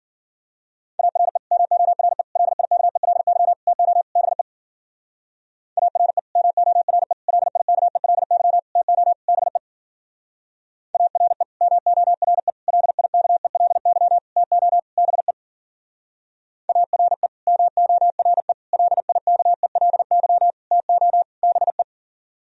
Morse code, heard as a keyed tone, about 50 wpm, 700 Hz. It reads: ARE MORE LIKELY TO BE ARE MORE LIKELY TO BE ARE MORE LIKELY TO BE ARE MORE LIKELY TO BE
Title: Speed-Racing - Sets of 5 Words 40wpm